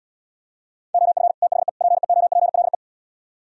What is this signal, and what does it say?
Morse code, heard as a keyed tone, about 77 wpm, 700 Hz. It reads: OF THE PEOPLE